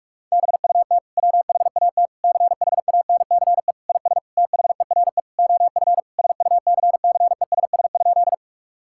Morse code, heard as a keyed tone, about 46 wpm, 700 Hz. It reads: BUT WHAT CHANCE IS THERE OF SUCCESS?
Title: Sentences from Top 1000 Words - Review Part 6 - 45wpm